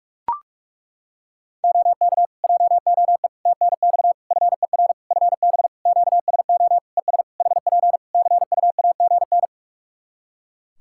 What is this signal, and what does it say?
Morse code, heard as a keyed tone, about 42 wpm, 700 Hz. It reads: OK JOE TNX FER FB QSO ES HP CUAGN
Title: K1USN SST QSO Practice 45wpm